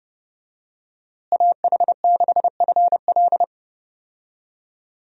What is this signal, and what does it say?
Morse code, heard as a keyed tone, about 30 wpm, 700 Hz. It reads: AH6FL
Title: USA Call Signs - 5 Characters - Repeated 3x - 30wpm